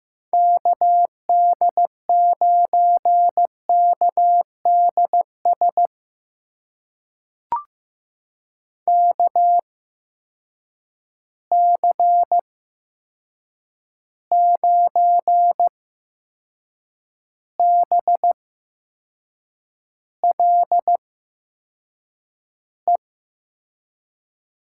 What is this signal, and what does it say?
Morse code, heard as a keyed tone, about 15 wpm, spaced as for 6 wpm, 700 Hz. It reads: KD9KDS KC9BLE
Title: USA Call Signs - 6 Characters - 8x Character Spacing - 15wpm